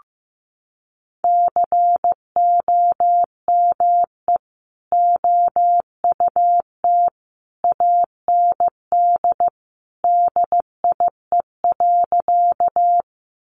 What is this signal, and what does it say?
Morse code, heard as a keyed tone, about 15 wpm, 700 Hz. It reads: COME OUT AND DIE.
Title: Sentences from Top 1300 Words 15wpm